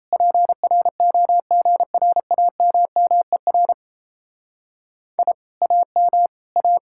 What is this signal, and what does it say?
Morse code, heard as a keyed tone, about 32 wpm, 700 Hz. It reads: PROGRAMMER I AM A
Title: Speed-Racing Common QSO Phrases Encore - 28wpm